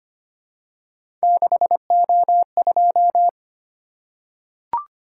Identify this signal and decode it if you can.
Morse code, heard as a keyed tone, about 25 wpm, 700 Hz. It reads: 6O2
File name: Three Letter-Number Combinations - 25wpm